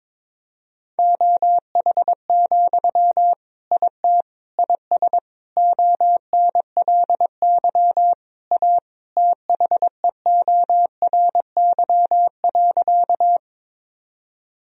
Morse code, heard as a keyed tone, about 22 wpm, 700 Hz. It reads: OH, IT IS ONLY A THEORY.
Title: Sentences from Top 1000 Words - Review Part 3 - 22wpm